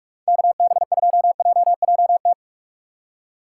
Morse code, heard as a keyed tone, about 46 wpm, 700 Hz. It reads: KB2JJT
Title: USA Call Signs - 6 Characters - 2x Character Spacing - 45wpm